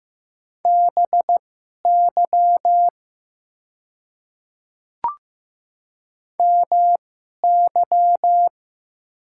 Morse code, heard as a keyed tone, about 15 wpm, spaced as for 11 wpm, 690 Hz. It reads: BY MY
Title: 2x Character Spacing - Top 100 Words - 15wpm